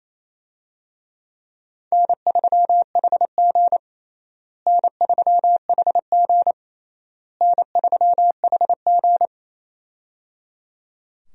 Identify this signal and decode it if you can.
Morse code, heard as a keyed tone, about 27 wpm, 700 Hz. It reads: N3HG N3HG N3HG
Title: USA Call Signs - 4 Characters - Repeated 3x - 28wpm